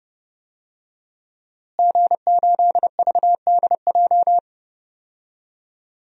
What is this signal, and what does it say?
Morse code, heard as a keyed tone, about 30 wpm, 700 Hz. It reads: G8VDJ
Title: DXCC-100 Call Signs - 5 Characters - Repeated 2x - 30wpm